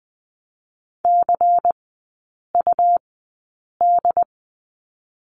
Morse code, heard as a keyed tone, about 20 wpm, 700 Hz. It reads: C U D
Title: 100 Lesson 17e - Letter u - Three Characters 20wpm